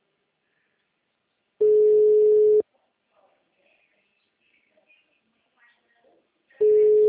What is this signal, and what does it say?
Ringback tone (Europe, 425 Hz), with speech.